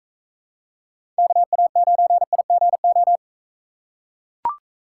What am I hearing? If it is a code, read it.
Morse code, heard as a keyed tone, about 41 wpm, 700 Hz. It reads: KA9IGO